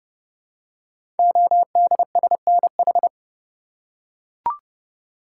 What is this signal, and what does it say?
Morse code, heard as a keyed tone, about 28 wpm, 700 Hz. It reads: ODSNH
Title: Code Groups - 5 Letter:Numbers - 30wpm